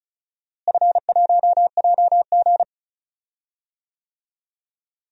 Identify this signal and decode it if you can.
Morse code, heard as a keyed tone, about 36 wpm, 700 Hz. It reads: F1JG
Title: DXCC-100 Call Signs - 4 Characters - Repeated 3x - 35wpm